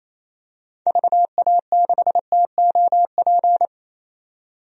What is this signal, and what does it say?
Morse code, heard as a keyed tone, about 28 wpm, 700 Hz. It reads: VA6TOP
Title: DXCC-100 Call Signs - 6 Characters - 28wpm